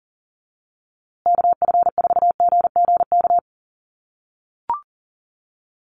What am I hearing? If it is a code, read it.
Morse code, heard as a keyed tone, about 40 wpm, 700 Hz. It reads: KF4GGK
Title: USA Call Signs Repeated 2x 40wpm